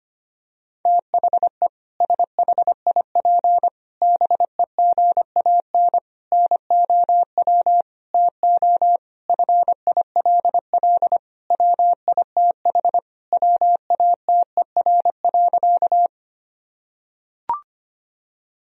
Morse code, heard as a keyed tone, about 25 wpm, 700 Hz. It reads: THE SHIP BEGAN NOW TO FILL WITH WATER.
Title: Sentences from Top 1000 Words - Review Part 3 - 25wpm